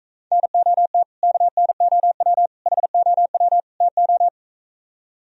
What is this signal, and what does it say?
Morse code, heard as a keyed tone, about 45 wpm, 700 Hz. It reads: NOT KNOW HOW TO